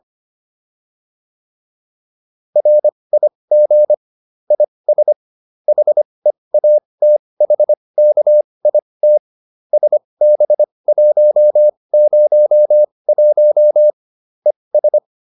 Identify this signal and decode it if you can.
Morse code, heard as a keyed tone, about 25 wpm, spaced as for 20 wpm, 600 Hz. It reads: RIG IS HEATHKIT SB101 ES
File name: QSO310a 20